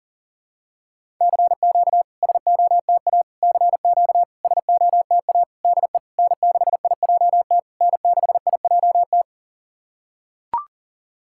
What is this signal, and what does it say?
Morse code, heard as a keyed tone, about 40 wpm, 700 Hz. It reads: CQ SOTA CQ SOTA DE N6IJT N6IJT
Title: SOTA QSO Practice 40wpm